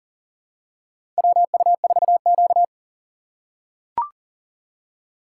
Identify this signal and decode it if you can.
Morse code, heard as a keyed tone, about 38 wpm, 700 Hz. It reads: WU4Q
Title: USA Call Signs - 4 Characters - 4x Character Spacing - 40wpm